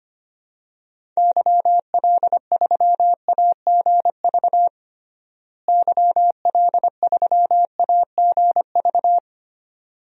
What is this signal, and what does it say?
Morse code, heard as a keyed tone, about 25 wpm, 700 Hz. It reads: YL3AGV YL3AGV
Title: DXCC-100 Call Signs - 6 Characters - Repeated 2x - 25wpm